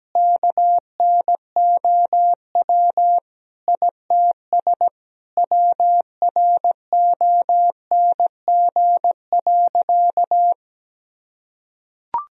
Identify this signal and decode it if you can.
Morse code, heard as a keyed tone, about 17 wpm, 700 Hz. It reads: KNOW ITS WRONG.
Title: Sentences from Top 1000 Words - Review Part 1 - 17wpm